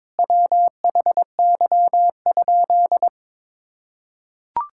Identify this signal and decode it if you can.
Morse code, heard as a keyed tone, about 21 wpm, 700 Hz. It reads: WHY?
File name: practice / 22wpm / Sentences with 4-Word Phrases - 22wpm